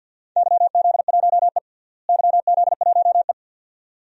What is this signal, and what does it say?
Morse code, heard as a keyed tone, about 50 wpm, 700 Hz. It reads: YZ1E YZ1E
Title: DXCC-100 Call Signs - 4 Characters - Repeated 2x - 50wpm